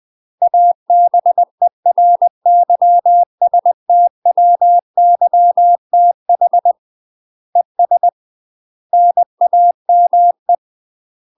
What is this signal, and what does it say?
Morse code, heard as a keyed tone, about 20 wpm, 700 Hz. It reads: ABERYSTWYTH ES NAME
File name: MASTER QSOs TRAINING/20WPM